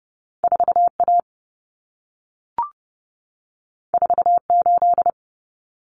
Morse code, heard as a keyed tone, about 30 wpm, 700 Hz. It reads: 4A 48
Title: Common Cut Numbers - 2-Digit 30wpm